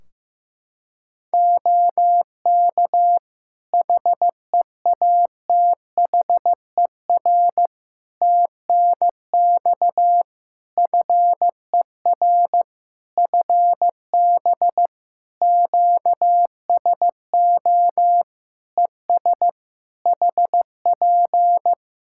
Morse code, heard as a keyed tone, about 15 wpm, 700 Hz. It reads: OK HEATHER TNX FER FB QSO ES HP
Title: K1USN SST QSO Practice 15wpm